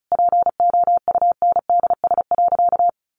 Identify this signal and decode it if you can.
Morse code, heard as a keyed tone, about 34 wpm, 700 Hz. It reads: POUNDS.
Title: Easy Phrases 35wpm